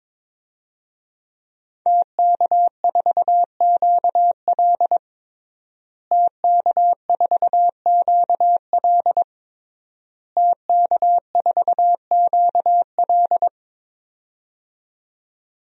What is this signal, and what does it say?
Morse code, heard as a keyed tone, about 22 wpm, 700 Hz. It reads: TK4QL TK4QL TK4QL
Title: DXCC-100 Call Signs - 5 Characters - Repeated 3x - 22wpm